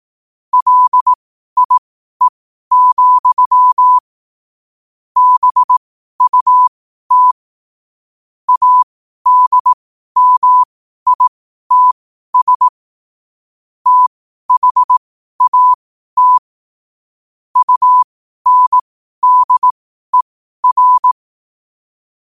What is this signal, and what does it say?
Morse code, heard as a keyed tone, about 18 wpm, spaced as for 12 wpm, 1000 Hz. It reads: LIE, BUT ADMITS THAT UNDER
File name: qotdfiles / 12wpm